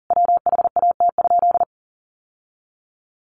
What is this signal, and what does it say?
Morse code, heard as a keyed tone, about 37 wpm, 700 Hz. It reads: WHAT?
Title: Sentences from Top 1100 Words 40wpm